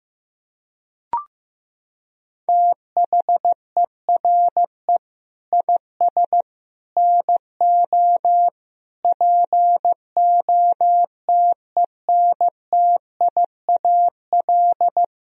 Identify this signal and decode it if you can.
Morse code, heard as a keyed tone, about 15 wpm, 700 Hz. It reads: THERE IS NO POTENTIAL
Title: Sentences from Top 1000 Words - Review Part 4 - 15wpm